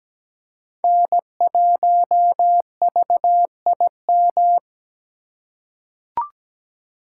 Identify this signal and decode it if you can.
Morse code, heard as a keyed tone, about 17 wpm, 700 Hz. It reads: N1VIM
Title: USA Call Signs - 5 Characters - 2x Character Spacing - 17wpm